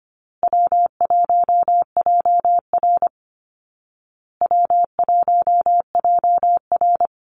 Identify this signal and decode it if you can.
Morse code, heard as a keyed tone, about 24 wpm, 700 Hz. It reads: W1JR W1JR
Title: Call Signs Repeated 2x - 25wpm